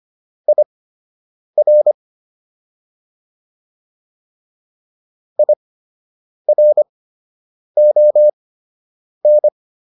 Morse code, heard as a keyed tone, about 25 wpm, spaced as for 8 wpm, 600 Hz. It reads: IR IRON